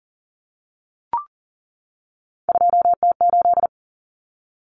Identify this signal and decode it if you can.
Morse code, heard as a keyed tone, about 40 wpm, 700 Hz. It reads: 2T8
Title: Common Cut Numbers - 3-Digit 40wpm